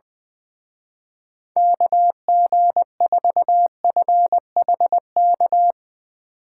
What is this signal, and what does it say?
Morse code, heard as a keyed tone, about 21 wpm, 700 Hz. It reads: KG4FHK
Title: Speed Racing - USA 2x3 Call Signs 20wpm